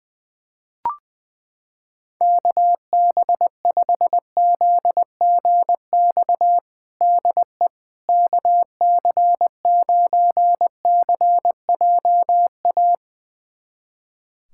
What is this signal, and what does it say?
Morse code, heard as a keyed tone, about 20 wpm, 700 Hz. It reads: KB5ZGX DE KC9CJA
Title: USA Callsign DE Callsign - 6 Characters - 20wpm